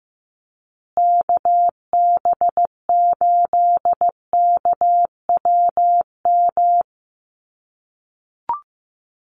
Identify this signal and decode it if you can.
Morse code, heard as a keyed tone, about 14 wpm, 700 Hz. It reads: KB8KWM
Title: USA Call Signs Repeated 3x 15wpm